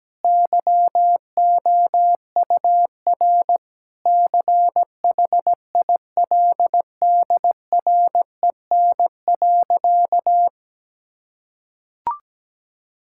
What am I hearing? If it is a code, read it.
Morse code, heard as a keyed tone, about 17 wpm, 700 Hz. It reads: YOUR CHILDREN.
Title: Sentences from Top 1000 Words - Review Part 6 - 17wpm